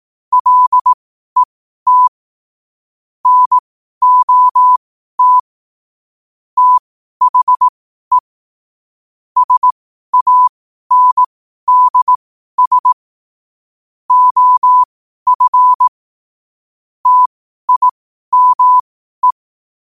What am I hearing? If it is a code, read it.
Morse code, heard as a keyed tone, about 18 wpm, spaced as for 12 wpm, 1000 Hz. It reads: LET NOT THE SANDS OF TIME
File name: qotdfiles / 12wpm